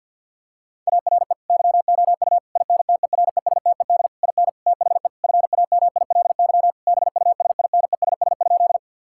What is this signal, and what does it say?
Morse code, heard as a keyed tone, about 50 wpm, 700 Hz. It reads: ARE YOU INTERESTED IN THE FAMILY BUSINESS?
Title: Sentences from Top 1000 Words - Review Part 6 - 50wpm